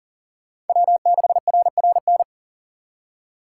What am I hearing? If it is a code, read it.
Morse code, heard as a keyed tone, about 50 wpm, 700 Hz. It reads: W6RRN